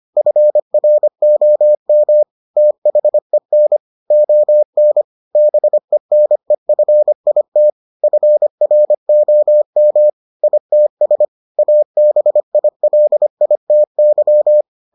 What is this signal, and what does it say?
Morse code, heard as a keyed tone, about 25 wpm, 600 Hz. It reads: FROM THEN ON BENEFIT FROM ITS ABILITY